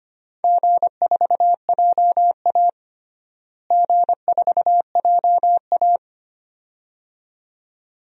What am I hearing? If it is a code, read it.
Morse code, heard as a keyed tone, about 25 wpm, 700 Hz. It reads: G4JA G4JA
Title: DXCC-100 Call Signs - 4 Characters - Repeated 2x - 25wpm